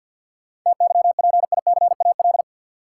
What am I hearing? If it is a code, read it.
Morse code, heard as a keyed tone, about 53 wpm, 700 Hz. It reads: TYPICAL
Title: Sentences from Top 1100 Words - New Word Drill Alt - 50wpm